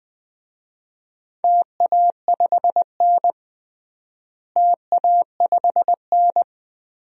Morse code, heard as a keyed tone, about 20 wpm, 700 Hz. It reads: TA5N TA5N
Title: DXCC-100 Call Signs - 4 Characters - Repeated 3x - 20wpm